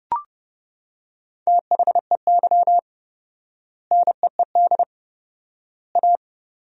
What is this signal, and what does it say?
Morse code, heard as a keyed tone, about 29 wpm, 700 Hz. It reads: THEY NEED A
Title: 4x Word-Spacing - Sentences from Top 500 Words - 30wpm